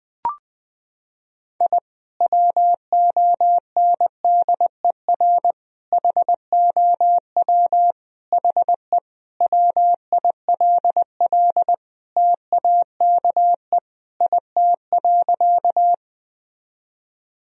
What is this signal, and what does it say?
Morse code, heard as a keyed tone, about 20 wpm, 700 Hz. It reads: I WONDER HOW HE WILL TAKE IT.
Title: Sentences from Top 1000 Words - Review Part 3 - 20wpm